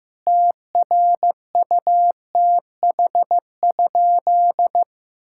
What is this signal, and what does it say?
Morse code, heard as a keyed tone, about 16 wpm, 700 Hz. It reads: TRUTH?
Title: Sentences from Top 1600 Words 15wpm